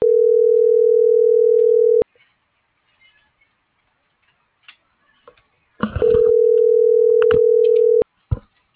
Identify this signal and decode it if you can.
Ringback tone (North America), 440 + 480 Hz.